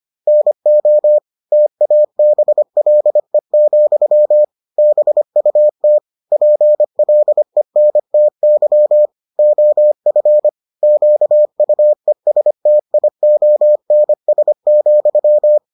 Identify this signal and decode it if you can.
Morse code, heard as a keyed tone, about 25 wpm, 600 Hz. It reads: NO TABLE, BUT PLENTY OF QUESTIONS,